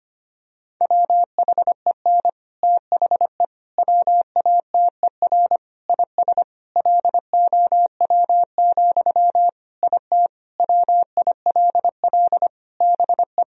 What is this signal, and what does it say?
Morse code, heard as a keyed tone, about 25 wpm, 700 Hz. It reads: WHEN THE WATER IS LOW, IT WILL BE
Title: Sentences from Top 1000 Words - Review Part 1 - 25wpm